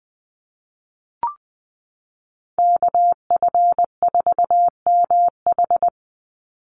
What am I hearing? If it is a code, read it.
Morse code, heard as a keyed tone, about 20 wpm, 700 Hz. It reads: KF4MH
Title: NA SOTA Chaser Callsigns 20wpm